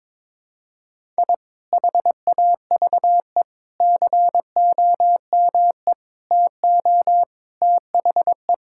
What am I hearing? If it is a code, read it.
Morse code, heard as a keyed tone, about 22 wpm, 700 Hz. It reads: I HAVE COME TO THE
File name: practice / 22wpm / Sets of 7 Words - 22wpm